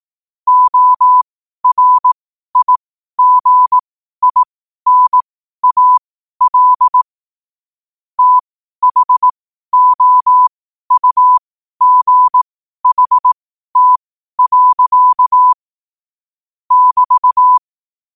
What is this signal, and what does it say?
Morse code, heard as a keyed tone, about 18 wpm, spaced as for 12 wpm, 1000 Hz. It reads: ORIGINAL THOUGHT. <BT>
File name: qotdfiles / 12wpm